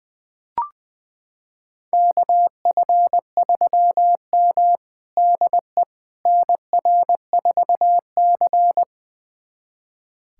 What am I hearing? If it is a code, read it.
Morse code, heard as a keyed tone, about 21 wpm, 700 Hz. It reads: KF3M DE NR4C
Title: USA Callsign DE Callsign - 4 Characters - 20wpm